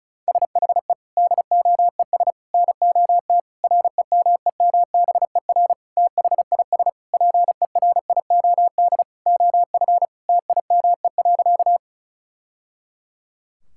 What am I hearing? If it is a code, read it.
Morse code, heard as a keyed tone, about 36 wpm, 700 Hz. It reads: SHE DOES NOT REMEMBER THIS PERIOD OF TIME.
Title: Sentences from Top 1000 Words - Review Part 5 - 35wpm